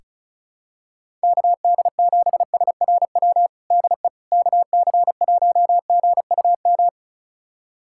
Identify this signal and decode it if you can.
Morse code, heard as a keyed tone, about 36 wpm, 700 Hz. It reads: KD7SRW DE KC1GUM
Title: USA Callsign DE Callsign - 6 Characters - 35wpm